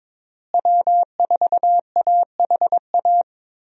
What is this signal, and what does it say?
Morse code, heard as a keyed tone, about 22 wpm, 700 Hz. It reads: W4AHA
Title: USA Call Signs - 5 Characters - Repeated 3x - 22wpm